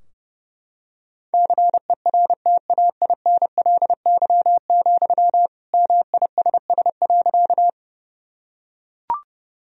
Morse code, heard as a keyed tone, about 32 wpm, 700 Hz. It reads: CERTAINLY, MISS.
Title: Sentences from Top 1000 Words - Review Part 3 - 30wpm